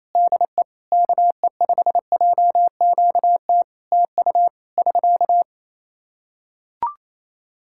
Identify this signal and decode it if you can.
Morse code, heard as a keyed tone, about 28 wpm, 700 Hz. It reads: DE KE5JQT TU <SK>
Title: Practice QSO - Exchange 4 - Ending - 28wpm